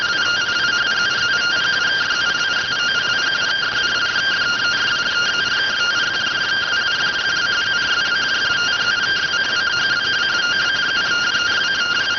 Two-tone teleprinter signal, about 170 Hz shift.